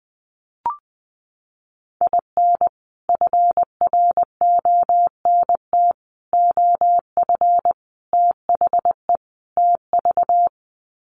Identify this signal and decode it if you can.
Morse code, heard as a keyed tone, about 20 wpm, 700 Hz. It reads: IN FRONT OF THE TV
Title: Sets of 5 Words Encore2 20wpm